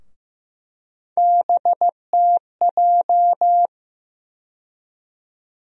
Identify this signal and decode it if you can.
Morse code, heard as a keyed tone, about 15 wpm, 700 Hz. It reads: BTJ
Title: USA Call Sign Postfixes - 3 Letters 15wpm